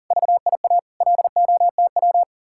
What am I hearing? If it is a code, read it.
Morse code, heard as a keyed tone, about 50 wpm, 700 Hz. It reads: VIA LOTW
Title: Speed-Racing Common QSO Phrases - 40wpm